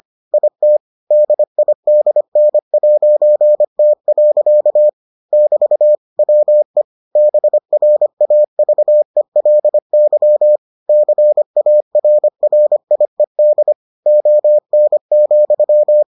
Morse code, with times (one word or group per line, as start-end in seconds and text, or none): 0.3-0.8 IT
1.1-4.9 DIDN'T.
5.3-6.0 <BT>
6.2-6.8 WE
7.2-10.6 BRAVELY
10.9-13.7 CARRIED
14.1-16.0 ON,